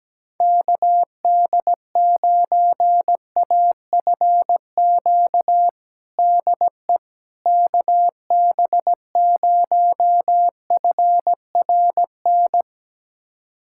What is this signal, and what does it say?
Morse code, heard as a keyed tone, about 17 wpm, 700 Hz. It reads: KD9AFQ DE KB0FRN
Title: USA Callsign DE Callsign - 6 Characters - 17wpm